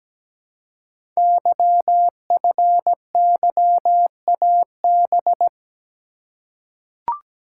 Morse code, heard as a keyed tone, about 17 wpm, 700 Hz. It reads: YFYAB
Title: Code Groups - 5 Letter:Numbers - 17wpm